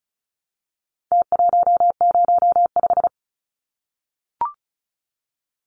Morse code, heard as a keyed tone, about 34 wpm, 700 Hz. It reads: T105
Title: Numbers - Contest Style - 4-Digit 35wpm